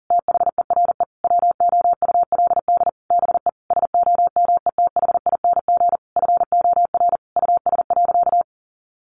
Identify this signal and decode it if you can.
Morse code, heard as a keyed tone, about 40 wpm, 700 Hz. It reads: THERE WOULD BE SOMETHING FOR US.
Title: Sentences from Top 1200 Words 40wpm